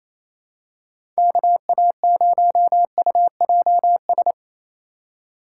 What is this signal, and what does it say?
Morse code, heard as a keyed tone, about 28 wpm, 700 Hz. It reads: KA0UJS
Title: USA Call Signs - 4x Character Spacing - 28wpm